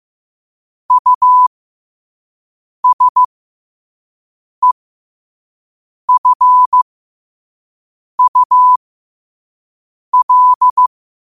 Morse code, heard as a keyed tone, about 15 wpm, spaced as for 5 wpm, 1000 Hz. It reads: USEFUL
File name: qotdfiles / 5wpm